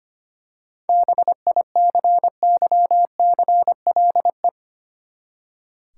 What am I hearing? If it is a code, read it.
Morse code, heard as a keyed tone, about 25 wpm, 700 Hz. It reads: BICYCLE
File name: practice / 25wpm / Easy Words with Hints - 25wpm